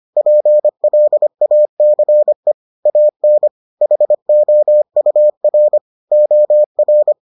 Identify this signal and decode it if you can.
Morse code, heard as a keyed tone, about 25 wpm, 600 Hz. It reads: PLACE AN HOUR OR